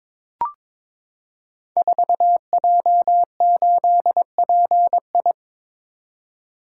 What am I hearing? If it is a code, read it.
Morse code, heard as a keyed tone, about 23 wpm, 700 Hz. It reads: 4J8PI
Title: Code Groups - 5 Letter:Numbers - 22wpm